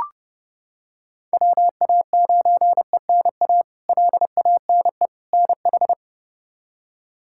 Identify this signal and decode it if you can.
Morse code, heard as a keyed tone, about 30 wpm, 700 Hz. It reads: WA9ENA LANE NH
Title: K1USN SST QSO Practice - 30wpm